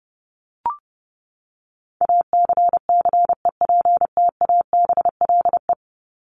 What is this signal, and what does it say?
Morse code, heard as a keyed tone, about 28 wpm, 700 Hz. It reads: ACCEPTABLE
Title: Top 1501-1750 Words 30wpm